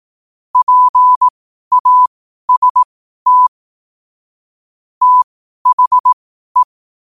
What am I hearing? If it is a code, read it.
Morse code, heard as a keyed tone, about 18 wpm, spaced as for 12 wpm, 1000 Hz. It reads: PAST THE
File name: qotdfiles / 12wpm